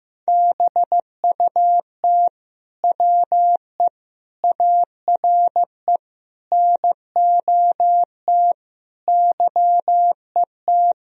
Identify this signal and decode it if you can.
Morse code, heard as a keyed tone, about 16 wpm, 700 Hz. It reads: BUT WE ARE NOT YET
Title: Sentences from Top 1000 Words - Review Part 1 - 15wpm